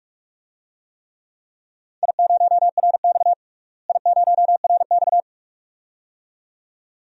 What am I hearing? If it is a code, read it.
Morse code, heard as a keyed tone, about 42 wpm, 700 Hz. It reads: I0RX I0RX
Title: DXCC-100 Call Signs - 4 Characters - Repeated 2x - 45wpm